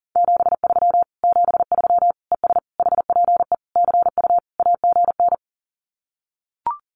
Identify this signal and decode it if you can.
Morse code, heard as a keyed tone, about 40 wpm, 700 Hz. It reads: 73 73 ES HPE CU AGN
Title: Common QSO Phrases Encore 40wpm